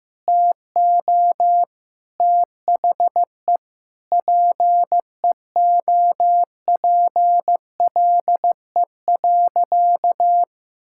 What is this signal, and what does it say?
Morse code, heard as a keyed tone, about 15 wpm, 700 Hz. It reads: TO THE PEOPLE.
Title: Sentences from Top 1000 Words - Review Part 6 - 15wpm